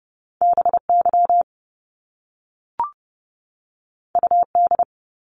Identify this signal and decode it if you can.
Morse code, heard as a keyed tone, about 30 wpm, 700 Hz. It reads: BY UD
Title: Top Two Letter Combinations in Words 30wpm